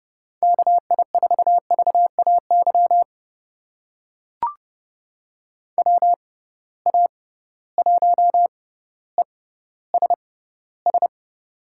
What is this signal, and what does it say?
Morse code, heard as a keyed tone, about 30 wpm, spaced as for 15 wpm, 700 Hz. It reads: KI4VAY WA1ESS
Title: USA Call Signs - 6 Characters - 6x Character Spacing - 30wpm